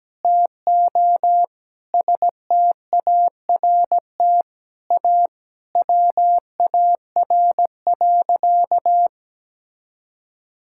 Morse code, heard as a keyed tone, about 17 wpm, 700 Hz. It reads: TO START A WAR.
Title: Sentences from Top 1000 Words - Review Part 6 - 17wpm